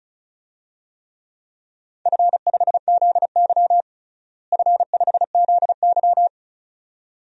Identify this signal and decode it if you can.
Morse code, heard as a keyed tone, about 34 wpm, 700 Hz. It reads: F5ZY F5ZY
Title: DXCC-100 Call Signs - 4 Characters - Repeated 2x - 35wpm